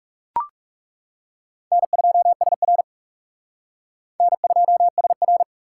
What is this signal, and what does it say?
Morse code, heard as a keyed tone, about 36 wpm, 700 Hz. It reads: N2SR N2SR
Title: Speed Racing - USA 1x2 Call Signs 30wpm